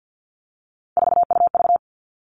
Morse code, heard as a keyed tone, about 50 wpm, 700 Hz. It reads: 4UV
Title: All Cut Numbers - 3-Digit 50wpm